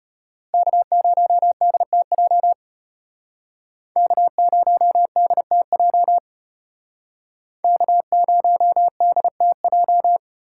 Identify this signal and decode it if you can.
Morse code, heard as a keyed tone, about 35 wpm, 700 Hz. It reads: K0DTJ K0DTJ K0DTJ